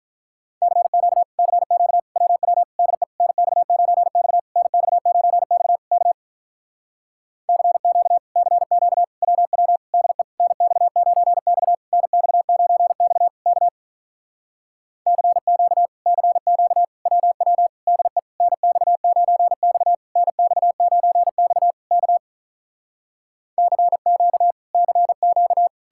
Morse code, heard as a keyed tone, about 47 wpm, 700 Hz. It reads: CQ CQ WW DE NX9X NX9X K CQ CQ WW DE NX9X NX9X K CQ CQ WW DE NX9X NX9X K CQ CQ